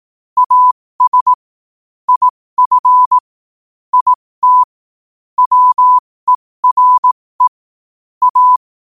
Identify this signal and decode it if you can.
Morse code, heard as a keyed tone, about 18 wpm, spaced as for 15 wpm, 1000 Hz. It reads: AS IF IT WERE A
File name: qotdfiles / 15wpm